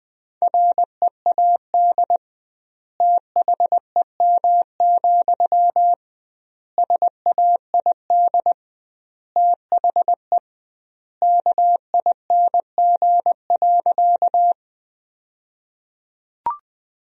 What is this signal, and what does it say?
Morse code, heard as a keyed tone, about 20 wpm, 700 Hz. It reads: READ THEM, SAID THE KING.
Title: 2x Word-Spacing - Sentences from Top 400 Words - 20wpm